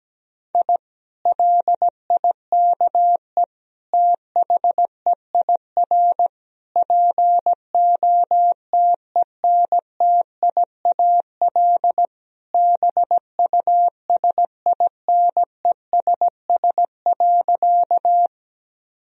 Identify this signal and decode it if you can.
Morse code, heard as a keyed tone, about 17 wpm, 700 Hz. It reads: I LIKE THEIR POTENTIAL BUSINESS.
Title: Sentences from Top 1000 Words - Review Part 4 - 17wpm